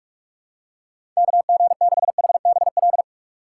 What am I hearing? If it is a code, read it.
Morse code, heard as a keyed tone, about 46 wpm, 700 Hz. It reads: KG6HBL
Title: USA Call Signs - 6 Characters - 4x Character Spacing - 45wpm